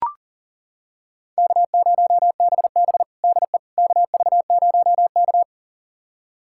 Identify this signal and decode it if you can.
Morse code, heard as a keyed tone, about 40 wpm, 700 Hz. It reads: K0BB DE KV0K
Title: USA Callsign DE Callsign - 4 Characters - 40wpm